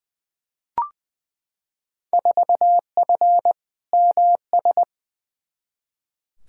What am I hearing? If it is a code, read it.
Morse code, heard as a keyed tone, about 20 wpm, 700 Hz. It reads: 4F MS
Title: Field Day Exchanges Encore - 20wpm